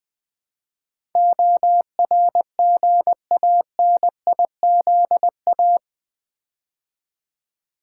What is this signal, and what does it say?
Morse code, heard as a keyed tone, about 20 wpm, 700 Hz. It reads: ORGANIZA
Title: Build A Word - 4 Letters At A Time - 12-Letter Words - 20wpm